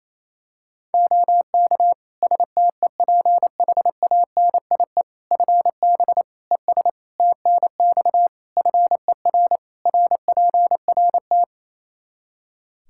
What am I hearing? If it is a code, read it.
Morse code, heard as a keyed tone, about 28 wpm, 700 Hz. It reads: OK STEPHANIE FB ES TNX FER RPRT
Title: Practice QSO - Exchange 2 - Info - 28wpm